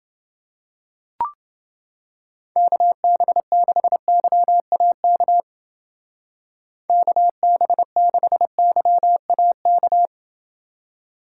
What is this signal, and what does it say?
Morse code, heard as a keyed tone, about 29 wpm, 700 Hz. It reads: KB6YAK KB6YAK